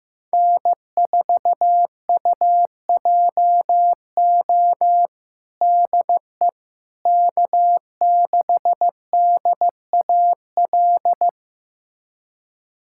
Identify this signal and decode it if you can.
Morse code, heard as a keyed tone, about 15 wpm, 700 Hz. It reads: N4UJO DE K6DAL
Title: USA Callsign DE Callsign - 5 Characters - 15wpm